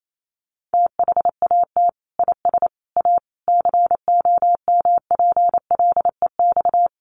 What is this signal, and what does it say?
Morse code, heard as a keyed tone, about 28 wpm, 700 Hz. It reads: THAT IS A COMPLEX
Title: Sentences from Top 900 Words 28wpm